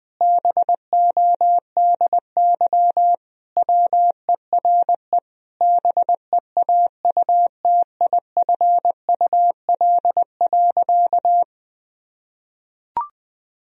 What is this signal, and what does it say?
Morse code, heard as a keyed tone, about 21 wpm, 700 Hz. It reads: BODY WERE BEAUTIFUL.
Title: Sentences from Top 1000 Words - Review Part 2 - 20wpm